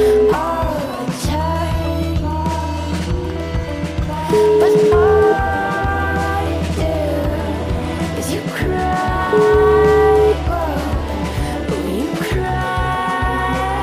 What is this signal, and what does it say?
Ringback tone (Europe, 425 Hz), with noise.